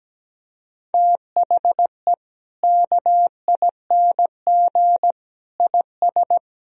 Morse code, heard as a keyed tone, about 17 wpm, 700 Hz. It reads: THE KING IS
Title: Sentences from Top 1000 Words - Review Part 4 - 17wpm